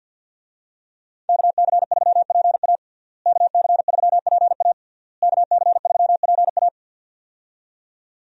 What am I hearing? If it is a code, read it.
Morse code, heard as a keyed tone, about 52 wpm, 700 Hz. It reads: KC3PA KC3PA KC3PA
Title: USA Call Signs - 5 Characters - Repeated 3x - 50wpm